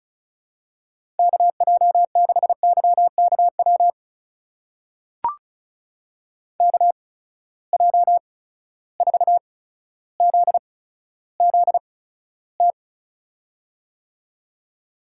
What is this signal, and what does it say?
Morse code, heard as a keyed tone, about 36 wpm, spaced as for 15 wpm, 700 Hz. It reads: KJ6YKW KJ4ZZT
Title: USA Call Signs - 6 Characters - 8x Character Spacing - 35wpm